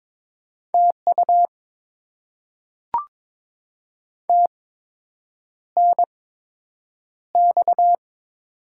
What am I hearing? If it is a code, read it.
Morse code, heard as a keyed tone, about 22 wpm, spaced as for 8 wpm, 700 Hz. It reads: TU TNX
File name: Top 20 QSO Elements - 8x Character Spacing - 22wpm